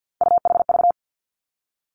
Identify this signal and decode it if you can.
Morse code, heard as a keyed tone, about 50 wpm, 700 Hz. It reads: UHV
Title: Three Challenging Letter-Number Combinations 50wpm